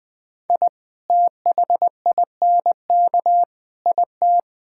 Morse code, heard as a keyed tone, about 20 wpm, 700 Hz. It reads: I THINK IT
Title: Sentences with 8-Word Phrases - 20wpm